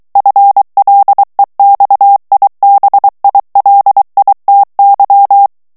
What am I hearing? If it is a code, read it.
Morse code, heard as a keyed tone, about 23 wpm, 800 Hz. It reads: FLEXIBILITY
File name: ANGLAIS texte clair 3 vitesse 22 mots minute